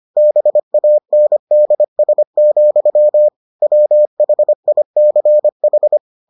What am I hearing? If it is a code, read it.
Morse code, heard as a keyed tone, about 25 wpm, 600 Hz. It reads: BANDS, WHICH